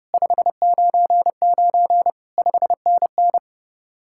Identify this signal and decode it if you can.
Morse code, heard as a keyed tone, about 29 wpm, 700 Hz. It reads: 599 5NN